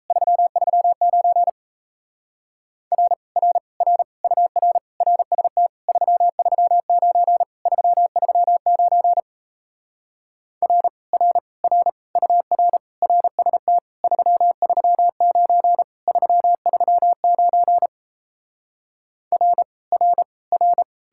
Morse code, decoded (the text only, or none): 339 R R R UR RST 339 339 R R R UR RST 339 339 R R R